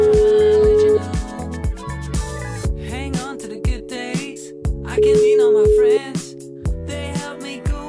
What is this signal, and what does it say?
Ringback tone (Europe, 425 Hz).